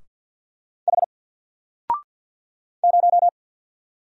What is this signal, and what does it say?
Morse code, heard as a keyed tone, about 50 wpm, 700 Hz. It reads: H 0
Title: Single Letter-Number Alt 50wpm